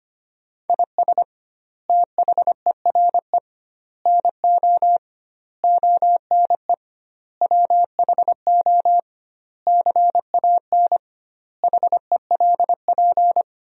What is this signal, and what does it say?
Morse code, heard as a keyed tone, about 25 wpm, 700 Hz. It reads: IS THERE NO ONE WHO CAN HELP
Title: 2x Word-Spacing - Sentences from Top 200 Words - 25wpm